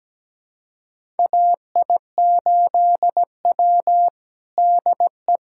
Morse code, heard as a keyed tone, about 17 wpm, 700 Hz. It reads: AI8W DE
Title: USA Callsign DE Callsign - 4 Characters - 17wpm